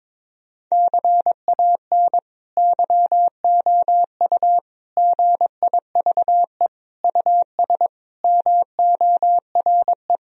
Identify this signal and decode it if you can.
Morse code, heard as a keyed tone, about 21 wpm, 700 Hz. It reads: CAN YOU GIVE US MORE
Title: Sentences from Top 1000 Words - Review Part 3 - 22wpm